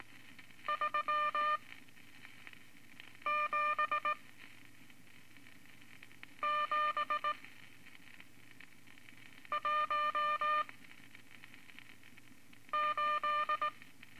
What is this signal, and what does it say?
Morse code, heard as a keyed tone, about 18 wpm, spaced as for 4 wpm, 1200 Hz. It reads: 37718
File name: Morse Code Records / 14B LCM 157 Units V Groups of Five